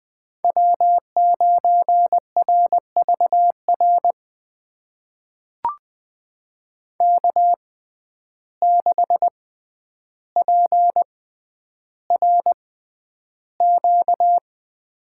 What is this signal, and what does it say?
Morse code, heard as a keyed tone, about 20 wpm, spaced as for 10 wpm, 700 Hz. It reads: W9RVR K6PRQ
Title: USA Call Signs - 5 Characters - 6x Character Spacing - 20wpm